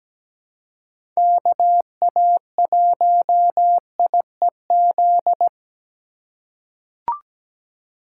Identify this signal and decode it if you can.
Morse code, heard as a keyed tone, about 17 wpm, 700 Hz. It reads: KA1IEZ
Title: USA Call Signs - 6 Characters - 2x Character Spacing - 17wpm